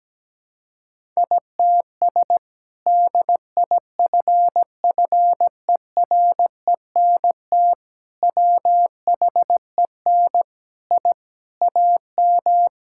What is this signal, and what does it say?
Morse code, heard as a keyed tone, about 17 wpm, 700 Hz. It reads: ITS DIFFERENT WHEN I AM
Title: Sentences from Top 1000 Words - Review Part 5 - 17wpm